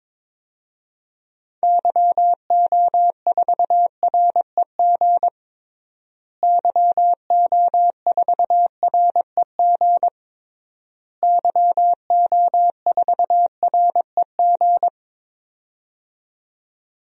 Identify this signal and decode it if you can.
Morse code, heard as a keyed tone, about 22 wpm, 700 Hz. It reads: YO4REG YO4REG YO4REG
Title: DXCC-100 Call Signs - 6 Characters - Repeated 3x - 22wpm